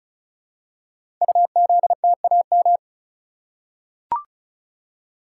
Morse code, heard as a keyed tone, about 36 wpm, 700 Hz. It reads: UZTAM
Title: Code Groups - 5 Letters - 35wpm